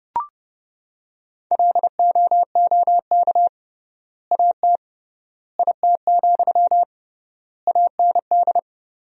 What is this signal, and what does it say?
Morse code, heard as a keyed tone, about 30 wpm, 700 Hz. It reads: LOOK AT IT, AND
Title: 3x Word-Spacing - Sentences from Top 200 Words - 30wpm